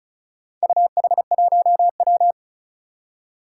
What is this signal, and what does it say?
Morse code, heard as a keyed tone, about 36 wpm, 700 Hz. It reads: UH1W
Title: Four Challenging Letter-Number Combinations 35wpm